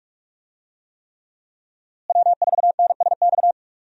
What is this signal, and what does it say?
Morse code, heard as a keyed tone, about 47 wpm, 700 Hz. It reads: W4NSX
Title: USA Call Signs - 5 Characters - Repeated 3x - 45wpm